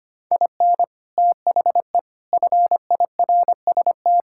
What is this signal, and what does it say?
Morse code, heard as a keyed tone, about 25 wpm, 700 Hz. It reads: IN THE FIRST
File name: practice / 25wpm / Sentences with 3-Word Phrases - 25wpm